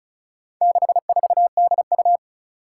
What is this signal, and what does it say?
Morse code, heard as a keyed tone, about 36 wpm, 700 Hz. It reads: 64DU
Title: Four Challenging Letter-Number Combinations 35wpm